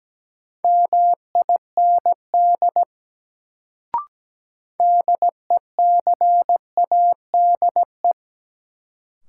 Morse code, heard as a keyed tone, about 17 wpm, 700 Hz. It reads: MIND DECADE
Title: Sets of 5 Words - New Word Drill - 17wpm